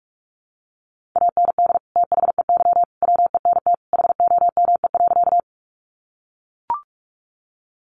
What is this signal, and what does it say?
Morse code, heard as a keyed tone, about 44 wpm, 700 Hz. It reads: AND THEY WENT HOME.
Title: Sentences from Top 200 Words 45wpm